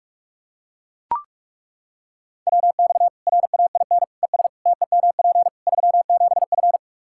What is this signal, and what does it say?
Morse code, heard as a keyed tone, about 41 wpm, 700 Hz. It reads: WX RAIN ES TEMP 37F
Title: Practice QSO - Exchange 2 - Info - 45wpm